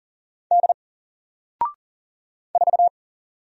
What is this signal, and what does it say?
Morse code, heard as a keyed tone, about 40 wpm, 700 Hz. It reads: D 4
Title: Single Character (Letter:Number:Punctuation:Prosign) Alt 40wpm